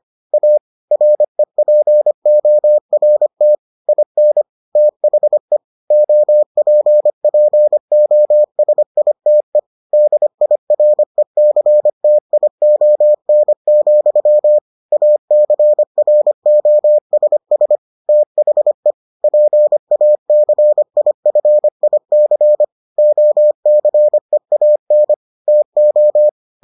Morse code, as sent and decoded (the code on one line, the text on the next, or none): .- / .-. .--. --- .-. - / .. -. / - .... / --- .--. .--. --- ... .. - . / -.. .. .-. -.-. - .. --- -. --..-- / .- -.-. .-. --- ... ... / - .... / .--. .- -.-. .. ..-. .. -.-. / --- -.-. .- -. / - ---
A REPORT IN THE OPPOSITE DIRECTION, ACROSS THE PACIFIC OCEAN TO